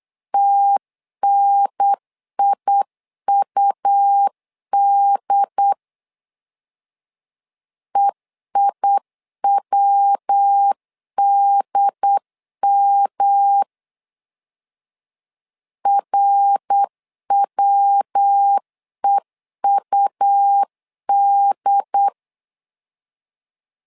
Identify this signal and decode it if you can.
Morse code, heard as a keyed tone, about 8 wpm, 800 Hz. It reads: TNIUD EIWDM RWEUD
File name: LEZIONI / 3 MWRO / 3 MWR 8wpm 2444